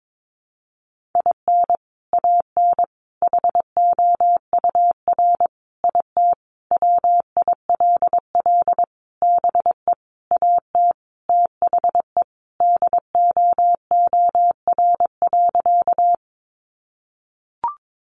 Morse code, heard as a keyed tone, about 22 wpm, 700 Hz. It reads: IN AN HOUR IT WILL BE AT THE DOOR.
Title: Sentences from Top 400 Words 22wpm